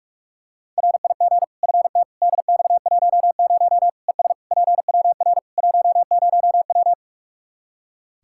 Morse code, heard as a keyed tone, about 46 wpm, 700 Hz. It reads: RIG FT DX10 ES PWR 10W
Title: Practice QSO - Exchange 2 - Info - 45wpm